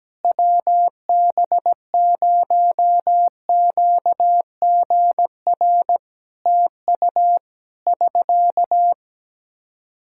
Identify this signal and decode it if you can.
Morse code, heard as a keyed tone, about 17 wpm, 700 Hz. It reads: WB0QGR TU <SK>
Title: Practice QSO - Exchange 4 - Ending - 17wpm